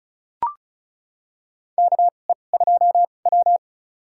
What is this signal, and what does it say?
Morse code, heard as a keyed tone, about 36 wpm, spaced as for 25 wpm, 700 Hz. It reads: KE2W
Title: USA Call Signs - 4 Characters - 2x Character Spacing - 35wpm